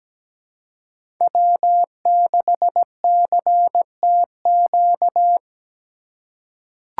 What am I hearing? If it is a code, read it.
Morse code, heard as a keyed tone, about 17 wpm, 700 Hz. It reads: W6CTQ
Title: USA Call Signs - 5 Characters - 3x Character Spacing - 17wpm